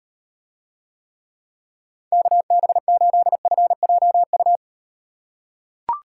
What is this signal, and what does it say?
Morse code, heard as a keyed tone, about 38 wpm, 700 Hz. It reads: KB8FJU